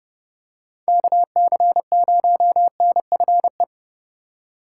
Morse code, heard as a keyed tone, about 30 wpm, 700 Hz. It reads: KC0NFE
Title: USA Call Signs - 6 Characters - 6x Character Spacing - 30wpm